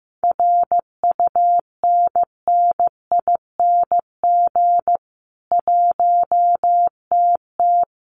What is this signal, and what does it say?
Morse code, heard as a keyed tone, about 15 wpm, 700 Hz. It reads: RUNNING 1TT
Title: Common QSO Phrases 15wpm